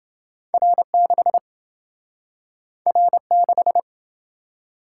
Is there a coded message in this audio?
Yes: Morse code from 0.5 to 3.8 s, heard as a keyed tone, about 27 wpm, 700 Hz. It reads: R6 R6